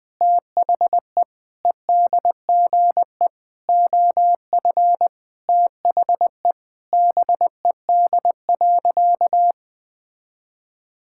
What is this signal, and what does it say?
Morse code, heard as a keyed tone, about 20 wpm, 700 Hz. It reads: THE EDGE OF THE BED.
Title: Sentences with 4-Word Phrases - 20wpm